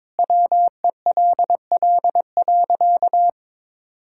Morse code, heard as a keyed tone, about 21 wpm, 700 Hz. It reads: WELL.
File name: practice / 22wpm / Easy Sentences with Hints - 22wpm